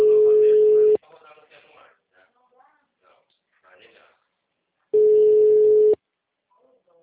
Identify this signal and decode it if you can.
Ringback tone (Europe, 425 Hz), with speech.